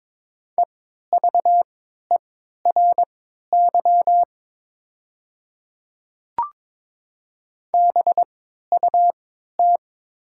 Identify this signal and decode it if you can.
Morse code, heard as a keyed tone, about 22 wpm, spaced as for 12 wpm, 700 Hz. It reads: EVERY BUT
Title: 3x Character Spacing - Top 300 Words - 22wpm